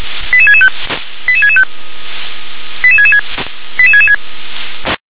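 Five-tone selective call (ZVEI), heard as quick runs of five tones, tones of about 70 ms. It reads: 88665 88665 88666 88666
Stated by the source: ZVEI five-tone call received with an SDR.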